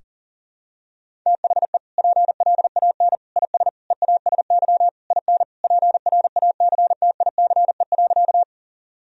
Morse code, heard as a keyed tone, about 40 wpm, 700 Hz. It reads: THE PLAN IS EASY IN PRACTICE.
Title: Sentences from Top 1000 Words - Review Part 4 - 40wpm